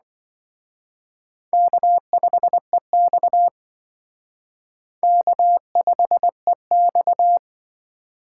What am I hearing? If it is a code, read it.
Morse code, heard as a keyed tone, about 23 wpm, 700 Hz. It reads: K5EX K5EX
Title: Speed Racing - USA 1x2 Call Signs 20wpm